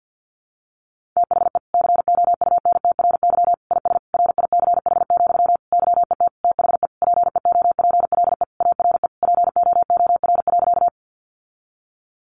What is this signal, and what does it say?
Morse code, heard as a keyed tone, about 50 wpm, 700 Hz. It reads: THE COUNTRY IS RICH, YET THE PEOPLE ARE POOR.
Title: Sentences from Top 1100 Words 50wpm